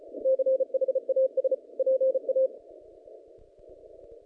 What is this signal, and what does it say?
Morse code, heard as a keyed tone, about 33 wpm, 530 Hz. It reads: CHAS PA
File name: contests / naqp-cw-k5tr-op-k5pi- -0300